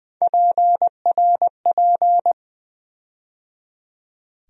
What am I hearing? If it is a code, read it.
Morse code, heard as a keyed tone, about 20 wpm, 700 Hz. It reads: PRP
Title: USA Call Sign Postfixes - 3 Letters 20wpm